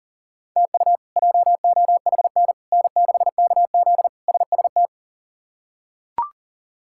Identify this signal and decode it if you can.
Morse code, heard as a keyed tone, about 40 wpm, 700 Hz. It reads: TU JOHN N6KZ SST
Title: K1USN SST QSO Practice - 40wpm